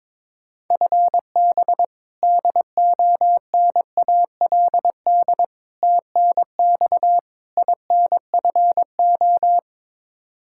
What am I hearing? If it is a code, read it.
Morse code, heard as a keyed tone, about 22 wpm, 700 Hz. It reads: FB DONALD TNX INFO
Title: Practice QSO - Exchange 3 - More Info - 22wpm